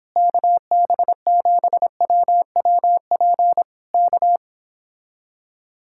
Morse code, heard as a keyed tone, about 26 wpm, 700 Hz. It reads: KB7WWP K
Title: Speed-Racing Common QSO Phrases - 22wpm